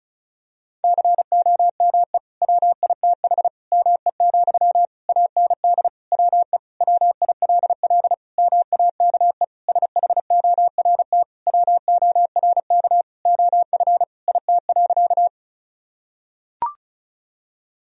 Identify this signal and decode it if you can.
Morse code, heard as a keyed tone, about 36 wpm, 700 Hz. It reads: COME WITH ME, AND WE WILL MAKE SHORT WORK OF IT.
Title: Sentences from Top 1000 Words - Review Part 2 - 35wpm